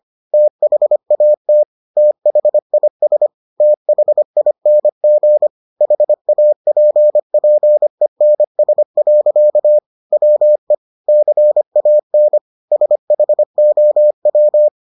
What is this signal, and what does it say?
Morse code, heard as a keyed tone, about 25 wpm, 600 Hz. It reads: THAT THIS THING HAPPENS. WE CAN SHOW